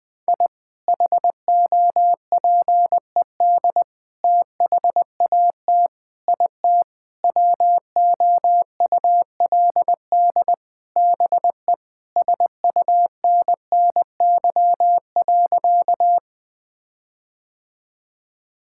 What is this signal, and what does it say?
Morse code, heard as a keyed tone, about 20 wpm, 700 Hz. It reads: I HOPED THAT IT WOULD BE SUNNY.
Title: Sentences with 4-Word Phrases - Spoken Beginning - 20wpm